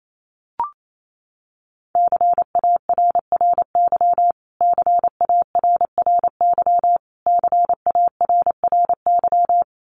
Morse code, heard as a keyed tone, about 28 wpm, 700 Hz. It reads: CARRY CARRY CARRY
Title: Top 300 Words Repeated 28wpm